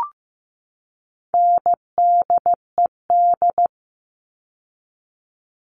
Morse code, heard as a keyed tone, about 15 wpm, 700 Hz. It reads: NDED
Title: All Cut Numbers - 4-Digit 15wpm